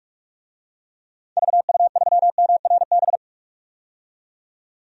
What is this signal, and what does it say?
Morse code, heard as a keyed tone, about 43 wpm, 700 Hz. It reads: VU3MRB
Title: DXCC-100 Call Signs - 6 Characters - 45wpm